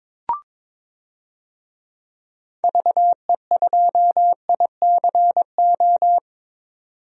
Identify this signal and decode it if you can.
Morse code, heard as a keyed tone, about 22 wpm, 700 Hz. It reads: VE2ICO
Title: DXCC-100 Call Signs - 6 Characters - 22wpm